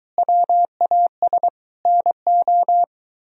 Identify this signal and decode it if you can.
Morse code, heard as a keyed tone, about 23 wpm, 700 Hz. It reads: WAS NO